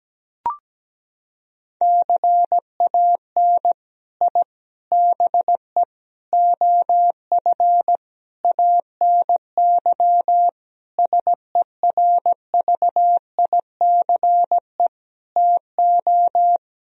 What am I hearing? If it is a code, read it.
Morse code, heard as a keyed tone, about 17 wpm, 700 Hz. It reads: CAN I BE OF ANY SERVICE TO
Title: Sentences from Top 1000 Words - Review Part 4 - 17wpm